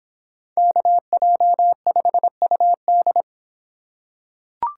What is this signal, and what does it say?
Morse code, heard as a keyed tone, about 26 wpm, 700 Hz. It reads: KJ5UD